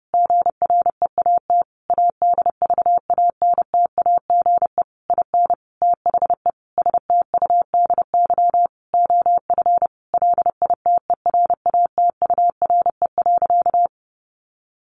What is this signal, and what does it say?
Morse code, heard as a keyed tone, about 28 wpm, 700 Hz. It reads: GREAT ADVANTAGE IN THE STUDY OF LITERATURE.
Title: Sentences from Top 900 Words 30wpm